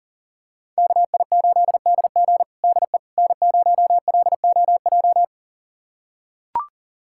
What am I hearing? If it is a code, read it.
Morse code, heard as a keyed tone, about 37 wpm, 700 Hz. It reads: KI8DG DE N0LOJ
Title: USA Callsign DE Callsign - 5 Characters - 40wpm